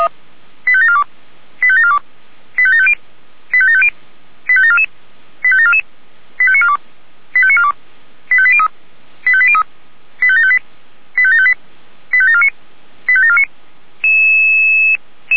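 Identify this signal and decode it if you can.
Five-tone selective call (ZVEI), heard as quick runs of five tones, tones of about 70 ms. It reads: 85742 85742 85750 85750 85744 85744 85942 85942 85803 85803 85758 85758 85749 85749
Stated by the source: ZVEI five-tone selective call, code 85724, received by SDR.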